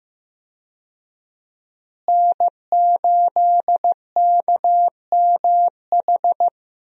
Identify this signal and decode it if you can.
Morse code, heard as a keyed tone, about 15 wpm, 700 Hz. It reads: N8KMH
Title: USA Call Signs - 5 Characters - Repeated 3x - 15wpm